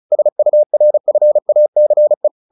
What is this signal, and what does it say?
Morse code, heard as a keyed tone, about 34 wpm, 600 Hz. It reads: SURFACE
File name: the old man and the sea / the old man and the sea 1 35wpm